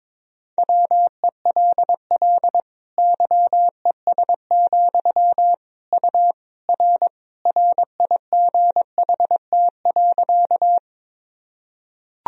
Morse code, heard as a keyed tone, about 21 wpm, 700 Hz. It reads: WELL YES, U R RIGHT.
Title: Sentences from Top 500 Words and Abbreviations 22wpm